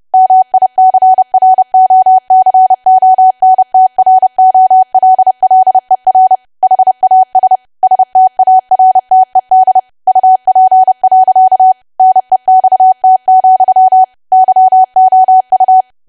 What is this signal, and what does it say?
Morse code, heard as a keyed tone, about 30 wpm, 750 Hz. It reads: MICROCONTROLLER HAS STARTED UP. NEXT, YOU
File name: cwa / W1AW-Level3 / 30WPM